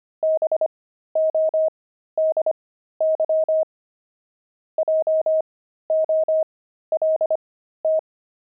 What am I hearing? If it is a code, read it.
Morse code, heard as a keyed tone, about 25 wpm, spaced as for 13 wpm, 630 Hz. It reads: BODY JOLT
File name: pangram-fs-13-cs-25-jolt